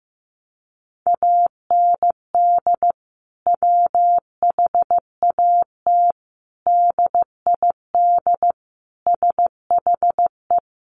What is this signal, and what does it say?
Morse code, heard as a keyed tone, about 15 wpm, 700 Hz. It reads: AND WHAT DID SHE
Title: Sentences from Top 200 Words 15wpm